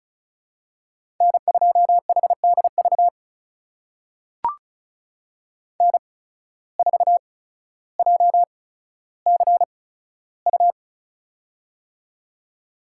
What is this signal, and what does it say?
Morse code, heard as a keyed tone, about 34 wpm, spaced as for 15 wpm, 700 Hz. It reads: N2HDV N4JCU
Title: USA Call Signs - 5 Characters - 8x Character Spacing - 35wpm